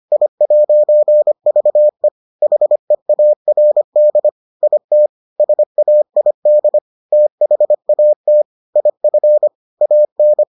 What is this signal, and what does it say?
Morse code, heard as a keyed tone, about 25 wpm, 600 Hz. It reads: I'VE HEARD IT SAID THAT IF AN